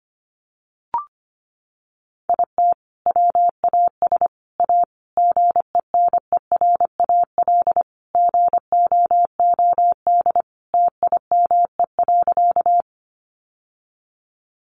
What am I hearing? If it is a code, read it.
Morse code, heard as a keyed tone, about 25 wpm, 700 Hz. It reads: IT WAS A GENERAL GOOD TIME.
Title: Sentences from Top 600 Words 25wpm